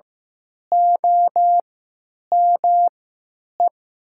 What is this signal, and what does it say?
Morse code, heard as a keyed tone, about 15 wpm, spaced as for 9 wpm, 700 Hz. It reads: OME
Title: 3x Character Spacing - Top 200 Words - 15wpm